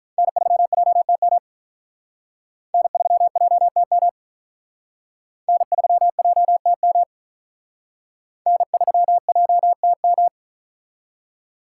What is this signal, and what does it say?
Morse code, heard as a keyed tone, about 43 wpm, 700 Hz. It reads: N3JTM N3JTM N3JTM N3JTM
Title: Speed Racing - USA 1x3 Call Signs 35wpm